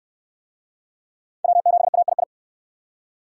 Morse code, heard as a keyed tone, about 65 wpm, 700 Hz. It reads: W7AII